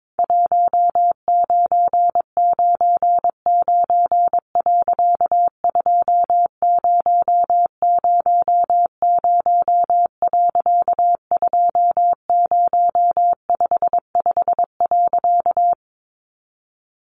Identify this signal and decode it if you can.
Morse code, heard as a keyed tone, about 22 wpm, 700 Hz. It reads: 1999.2000.2055.
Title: A Sound Of Thunder 22wpm